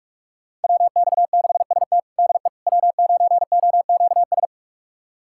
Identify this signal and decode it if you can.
Morse code, heard as a keyed tone, about 46 wpm, 700 Hz. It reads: WX6ST DE W9OQS
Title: USA Callsign DE Callsign - 5 Characters - 45wpm